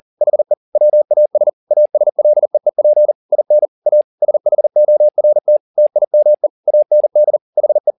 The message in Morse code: HE WAS ASLEEP IN A SHORT TIME AND HE

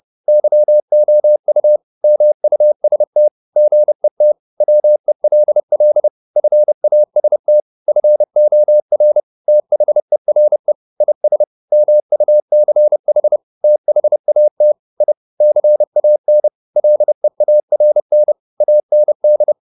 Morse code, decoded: YOU MUST GET WELL FAST FOR THERE IS MUCH THAT I CAN LEARN AND

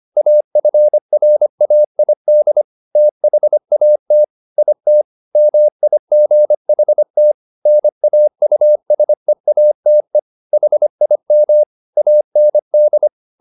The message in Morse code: AFRAID THAT IT MIGHT NAUSEATE HIM AND

